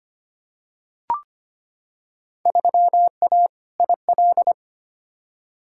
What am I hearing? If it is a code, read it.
Morse code, heard as a keyed tone, about 25 wpm, 700 Hz. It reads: 3A IL
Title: Field Day Exchanges - 25wpm